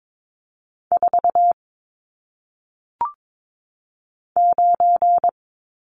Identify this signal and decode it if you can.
Morse code, heard as a keyed tone, about 22 wpm, 700 Hz. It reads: 4 9
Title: Numbers - 1-Digit 22wpm